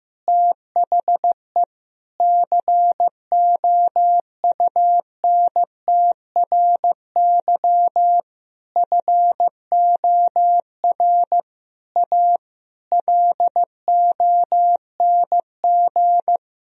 Morse code, heard as a keyed tone, about 15 wpm, 700 Hz. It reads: THE COUNTRY FOR A LONG
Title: Sentences from Top 1000 Words - Review Part 1 - 15wpm